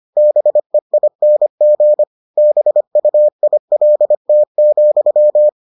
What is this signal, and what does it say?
Morse code, heard as a keyed tone, about 25 wpm, 600 Hz. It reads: BEING BUILT,